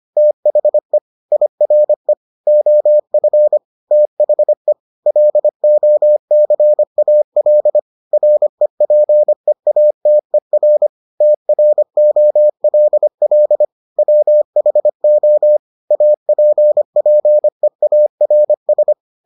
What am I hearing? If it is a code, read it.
Morse code, heard as a keyed tone, about 25 wpm, 600 Hz. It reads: THE IRE OF THE LOCAL REPEATER TROLL WHO APPEARS